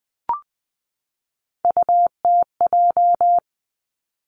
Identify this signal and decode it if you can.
Morse code, heard as a keyed tone, about 20 wpm, 700 Hz. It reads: UTJ
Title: Three Challenging Letter-Number Combinations 20wpm